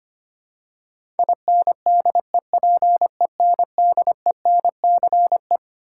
Morse code, heard as a keyed tone, about 25 wpm, 700 Hz. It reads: INDEPENDENCE
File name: Twelve Letter Words - 25wpm